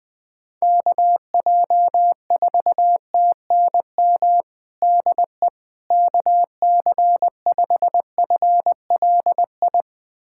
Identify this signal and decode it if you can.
Morse code, heard as a keyed tone, about 20 wpm, 700 Hz. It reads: KJ4TNM DE KC5FLI
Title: USA Callsign DE Callsign - 6 Characters - 20wpm